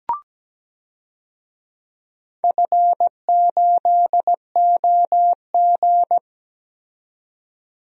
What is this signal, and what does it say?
Morse code, heard as a keyed tone, about 17 wpm, 700 Hz. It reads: F8OG
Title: DXCC-100 Call Signs - 4 Characters - 17wpm